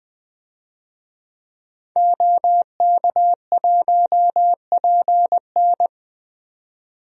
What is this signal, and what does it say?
Morse code, heard as a keyed tone, about 20 wpm, 700 Hz. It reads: OK1PN
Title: DXCC-100 Call Signs - 5 Characters - 20wpm